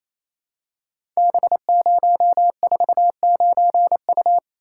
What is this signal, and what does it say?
Morse code, heard as a keyed tone, about 27 wpm, 700 Hz. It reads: B049U
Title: Code Groups - 5 Letter:Numbers - 28wpm